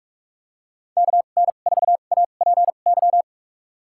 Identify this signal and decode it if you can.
Morse code, heard as a keyed tone, about 45 wpm, spaced as for 32 wpm, 700 Hz. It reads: KN4APY
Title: USA Call Signs - 6 Characters - 2x Character Spacing - 45wpm